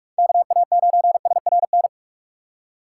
Morse code, heard as a keyed tone, about 46 wpm, 700 Hz. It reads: KA9SRN
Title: USA Call Signs - 6 Characters - 4x Character Spacing - 45wpm